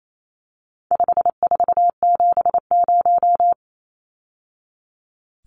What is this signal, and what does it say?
Morse code, heard as a keyed tone, about 28 wpm, 700 Hz. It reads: 5470
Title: Numbers - 4-Digit 28wpm